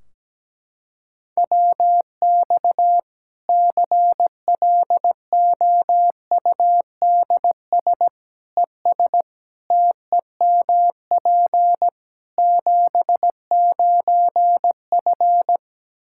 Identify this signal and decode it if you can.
Morse code, heard as a keyed tone, about 17 wpm, 700 Hz. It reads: WX CLOUDS ES TEMP 79F
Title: Practice QSO - Exchange 2 - Info - 17wpm